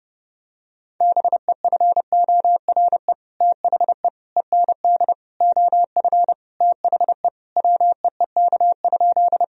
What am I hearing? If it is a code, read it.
Morse code, heard as a keyed tone, about 32 wpm, 700 Hz. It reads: BEFORE THE END OF THE WEEK?
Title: Sentences from Top 1000 Words - Review Part 2 - 30wpm